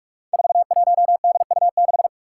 Morse code, heard as a keyed tone, about 44 wpm, 700 Hz. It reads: 41DU6
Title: Five Challenging Letter-Number Combinations 45wpm